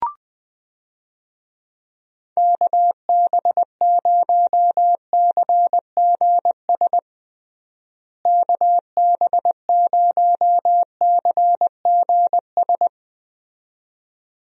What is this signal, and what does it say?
Morse code, heard as a keyed tone, about 21 wpm, 700 Hz. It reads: KB0CGS KB0CGS
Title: USA Call Signs - 6 Characters - Repeated 2x - 20wpm